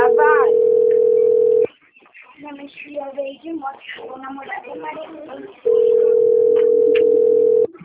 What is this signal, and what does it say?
Ringback tone (North America), 440 + 480 Hz, with speech.
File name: rg-1600-unknown- - - .28390